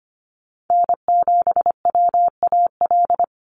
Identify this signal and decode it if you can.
Morse code, heard as a keyed tone, about 25 wpm, 700 Hz. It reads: N7WAL
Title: USA Call Signs 25wpm